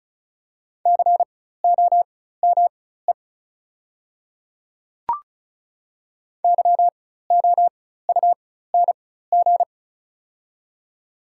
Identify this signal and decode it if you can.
Morse code, heard as a keyed tone, about 35 wpm, spaced as for 16 wpm, 700 Hz. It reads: COME YOUNG
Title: 4x Character Spacing - Top 500 Words - 35wpm